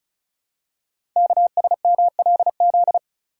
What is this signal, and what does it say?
Morse code, heard as a keyed tone, about 36 wpm, 700 Hz. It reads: KSMLZ
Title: Code Groups - 5 Letters - 35wpm